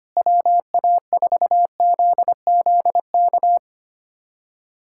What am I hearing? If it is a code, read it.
Morse code, heard as a keyed tone, about 25 wpm, 700 Hz. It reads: WA4ZZK
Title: USA Callsign DE Callsign - 6 Characters - 25wpm